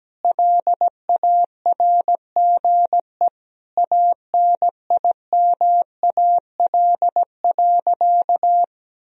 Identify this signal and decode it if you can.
Morse code, heard as a keyed tone, about 17 wpm, 700 Hz. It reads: LARGE ANIMAL.
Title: Sentences from Top 1000 Words - Review Part 1 - 17wpm